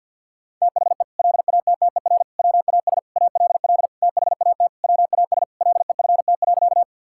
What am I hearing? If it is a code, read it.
Morse code, heard as a keyed tone, about 47 wpm, 700 Hz. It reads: THE LATTER WAS ALL THAT WAS LEFT.
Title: Sentences from Top 1600 Words 50wpm